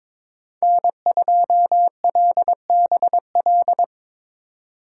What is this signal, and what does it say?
Morse code, heard as a keyed tone, about 22 wpm, 700 Hz. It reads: N2LBL
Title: USA Call Signs - 5 Characters - 2x Character Spacing - 22wpm